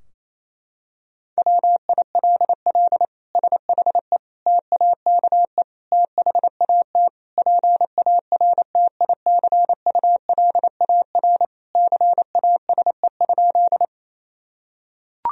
Morse code, heard as a keyed tone, about 28 wpm, 700 Hz. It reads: WILL SHE TAKE THAT PARTICULAR CASE?
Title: Sentences from Top 1000 Words - Review Part 4 - 28wpm